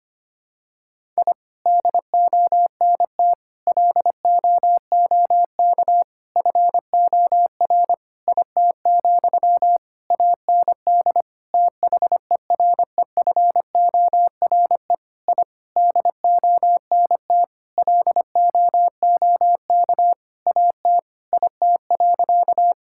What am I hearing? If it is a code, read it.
Morse code, heard as a keyed tone, about 25 wpm, 700 Hz. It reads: I DONT LOOK FOR IT, AND THEREFORE I DONT LOOK AT IT.
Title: Sentences from Top 1000 Words - Review Part 5 - 25wpm